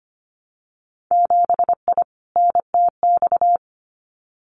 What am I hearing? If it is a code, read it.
Morse code, heard as a keyed tone, about 25 wpm, 690 Hz. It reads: 7I NTX
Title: Winter Field Day Exchanges Encore 25wpm